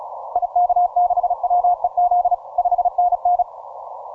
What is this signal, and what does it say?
Morse code, heard as a keyed tone, about 37 wpm, 710 Hz. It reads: IK6WEZ 5NN